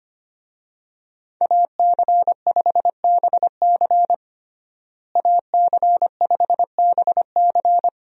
Morse code, heard as a keyed tone, about 25 wpm, 700 Hz. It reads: AC5BC AC5BC
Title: USA Call Signs - 5 Characters - Repeated 2x - 25wpm